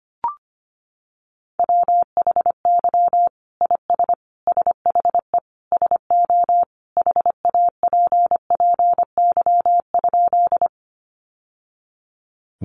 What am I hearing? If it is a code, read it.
Morse code, heard as a keyed tone, about 25 wpm, 700 Hz. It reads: WHY IS SHE SO HAPPY?
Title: Sentences from Top 1000 Words 25wpm